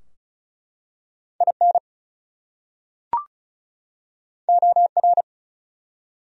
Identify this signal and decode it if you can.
Morse code, heard as a keyed tone, about 34 wpm, 700 Hz. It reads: IN OR
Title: Top 100 Words 35wpm